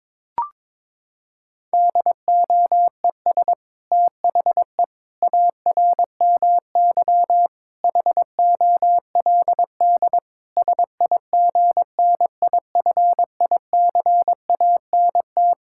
Morse code, heard as a keyed tone, about 22 wpm, 700 Hz. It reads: DOES THE ARMY HOLD SIGNIFICANT
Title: Sentences from Top 1000 Words - Review Part 6 - 22wpm